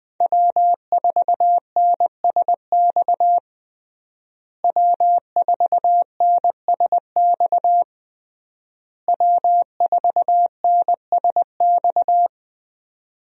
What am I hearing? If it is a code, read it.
Morse code, heard as a keyed tone, about 20 wpm, 700 Hz. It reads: W4NSX W4NSX W4NSX
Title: USA Call Signs - 5 Characters - Repeated 3x - 20wpm